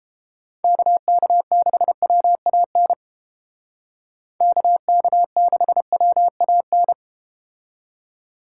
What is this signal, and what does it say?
Morse code, heard as a keyed tone, about 30 wpm, 690 Hz. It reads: KK6WAN KK6WAN